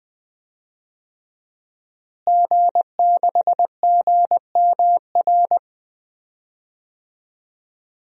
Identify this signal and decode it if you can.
Morse code, heard as a keyed tone, about 20 wpm, 700 Hz. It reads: G6GMR
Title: DXCC-100 Call Signs - 5 Characters - Repeated 2x - 20wpm